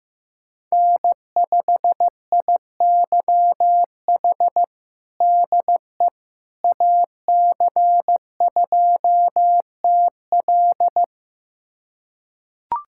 Morse code, heard as a keyed tone, about 16 wpm, 700 Hz. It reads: N5IYH DE AC2TL
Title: USA Callsign DE Callsign - 5 Characters - 15wpm